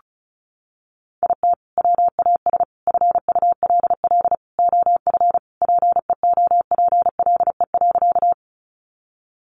Morse code, heard as a keyed tone, about 34 wpm, 700 Hz. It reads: IT WAS FULL OF PEOPLE.
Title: Sentences from Top 500 Words 35wpm